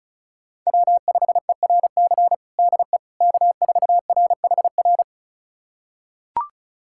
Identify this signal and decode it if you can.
Morse code, heard as a keyed tone, about 36 wpm, 700 Hz. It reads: W5ERC DE K4RHR
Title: USA Callsign DE Callsign 35wpm